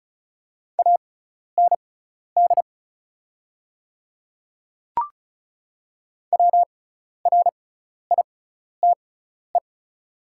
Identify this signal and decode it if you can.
Morse code, heard as a keyed tone, about 34 wpm, spaced as for 12 wpm, 690 Hz. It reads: AND WRITE
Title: 6x Character Spacing - Top 100 Words - 35wpm